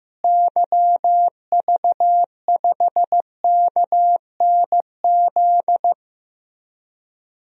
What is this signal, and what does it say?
Morse code, heard as a keyed tone, about 15 wpm, 700 Hz. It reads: YV5KNZ
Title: DXCC-100 Call Signs - 6 Characters - Repeated 3x - 15wpm